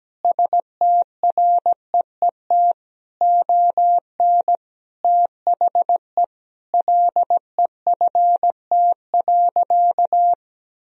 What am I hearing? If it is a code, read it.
Morse code, heard as a keyed tone, about 17 wpm, 700 Hz. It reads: STREET ON THE LEFT.
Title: Sentences from Top 1000 Words - Review Part 3 - 17wpm